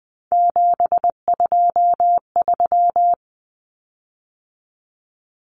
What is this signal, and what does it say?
Morse code, heard as a keyed tone, about 20 wpm, 700 Hz. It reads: 723
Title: Common Cut Numbers - 3-Digit 20wpm